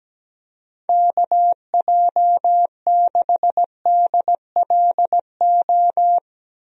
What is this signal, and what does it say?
Morse code, heard as a keyed tone, about 17 wpm, 700 Hz. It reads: KJ6DLO
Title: USA Call Signs - 6 Characters - Repeated 3x - 17wpm